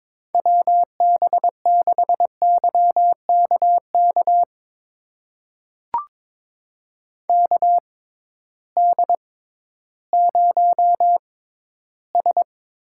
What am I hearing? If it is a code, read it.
Morse code, heard as a keyed tone, about 21 wpm, spaced as for 13 wpm, 700 Hz. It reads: WB6YKK KD0S
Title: USA Call Signs - 6 Characters - 6x Character Spacing - 22wpm